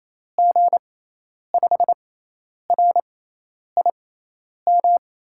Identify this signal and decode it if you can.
Morse code, heard as a keyed tone, about 28 wpm, spaced as for 10 wpm, 700 Hz. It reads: G5RIM
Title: USA Call Signs - 6 Characters - 6x Character Spacing - 28wpm